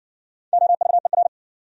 Morse code, heard as a keyed tone, about 58 wpm, 700 Hz. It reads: OVER